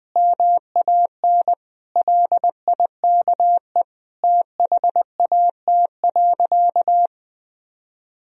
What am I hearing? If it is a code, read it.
Morse code, heard as a keyed tone, about 20 wpm, 700 Hz. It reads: MAN LIKE THAT.
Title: Sentences from Top 1000 Words - Review Part 4 - 20wpm